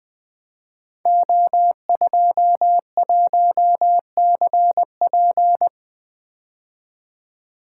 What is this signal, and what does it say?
Morse code, heard as a keyed tone, about 20 wpm, 700 Hz. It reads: O21CP
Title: Code Groups - 5 Letter:Numbers - 20wpm